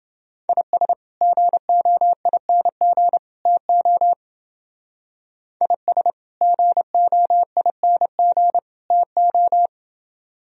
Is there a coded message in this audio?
Yes: Morse code from 0.5 to 9.7 s, heard as a keyed tone, about 29 wpm, 700 Hz. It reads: IS GOING TO IS GOING TO